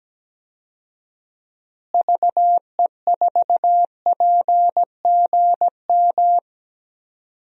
Morse code, heard as a keyed tone, about 17 wpm, 700 Hz. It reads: VE4PGM
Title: DXCC-100 Call Signs - 6 Characters - Repeated 2x - 17wpm